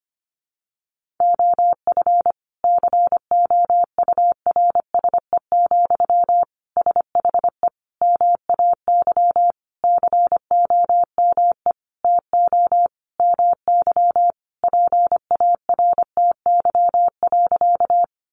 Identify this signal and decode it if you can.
Morse code, heard as a keyed tone, about 25 wpm, 700 Hz. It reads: OF COURSE, SHE MAY COME TO MY PARTY.
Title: Sentences from Top 1400 Words 25wpm